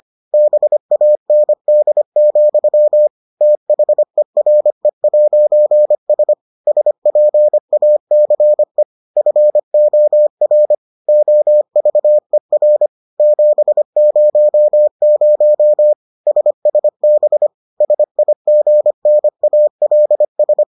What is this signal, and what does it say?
Morse code, heard as a keyed tone, about 25 wpm, 600 Hz. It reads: BAND, THERE'S SPACE FOR OVER 700 SSB SIGNALS